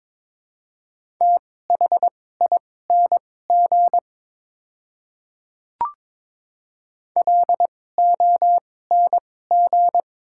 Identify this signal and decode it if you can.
Morse code, heard as a keyed tone, about 21 wpm, spaced as for 16 wpm, 700 Hz. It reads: THING LONG
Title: 2x Character Spacing - Top 300 Words - 22wpm